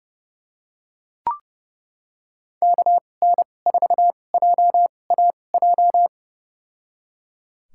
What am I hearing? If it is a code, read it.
Morse code, heard as a keyed tone, about 30 wpm, spaced as for 22 wpm, 700 Hz. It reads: KN4JAJ
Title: USA Call Signs - 6 Characters - 2x Character Spacing - 30wpm